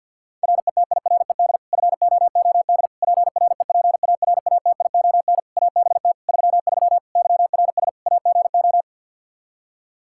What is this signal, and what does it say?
Morse code, heard as a keyed tone, about 52 wpm, 700 Hz. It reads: RETIRED FOOD PREPARATION ABT 33 YRS AGO
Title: K1USN SST QSO Practice 50wpm